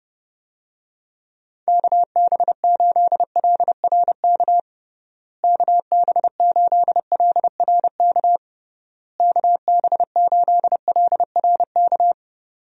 Morse code, heard as a keyed tone, about 32 wpm, 700 Hz. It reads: KB8LRK KB8LRK KB8LRK
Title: USA Call Signs - 6 Characters - Repeated 3x - 30wpm